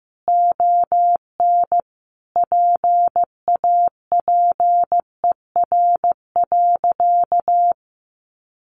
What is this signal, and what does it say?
Morse code, heard as a keyed tone, about 15 wpm, 700 Hz. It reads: ON PAPER.
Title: Sentences from Top 1200 Words 15wpm